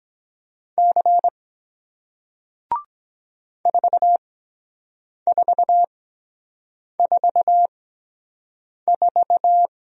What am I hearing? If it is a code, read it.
Morse code, heard as a keyed tone, about 22 wpm, spaced as for 7 wpm, 700 Hz. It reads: C 4444